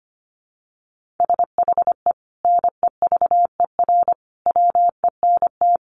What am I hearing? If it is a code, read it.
Morse code, heard as a keyed tone, about 25 wpm, 700 Hz. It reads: SHE NEVER WENT
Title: Sentences from Top 1400 Words 25wpm